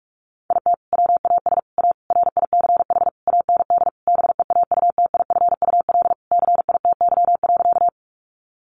Morse code, heard as a keyed tone, about 44 wpm, 700 Hz. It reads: IT WAS A RICH AND BEAUTIFUL CITY.
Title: Sentences from Top 1100 Words 45wpm